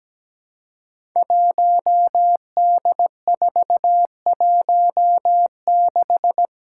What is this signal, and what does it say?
Morse code, heard as a keyed tone, about 17 wpm, 700 Hz. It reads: 1D416
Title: Five Challenging Letter-Number Combinations 17wpm